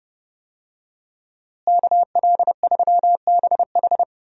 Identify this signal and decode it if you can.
Morse code, heard as a keyed tone, about 30 wpm, 700 Hz. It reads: KL3BH
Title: USA Call Signs - 5 Characters - Repeated 3x - 30wpm